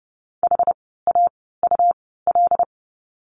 Morse code, heard as a keyed tone, about 30 wpm, spaced as for 17 wpm, 700 Hz. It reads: HAUL
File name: Lesson 17f - Letter u - Words 30wpm